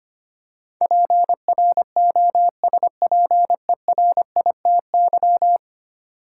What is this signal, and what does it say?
Morse code, heard as a keyed tone, about 25 wpm, 700 Hz. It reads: PROSPERITY
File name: 10 Letter Words - First 3 Letters Given - 25wpm